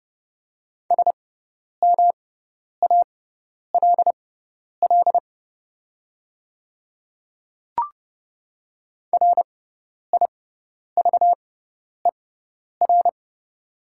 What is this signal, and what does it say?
Morse code, heard as a keyed tone, about 30 wpm, spaced as for 10 wpm, 700 Hz. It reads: SMALL RIVER
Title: 6x Character Spacing - Top 300 Words - 30wpm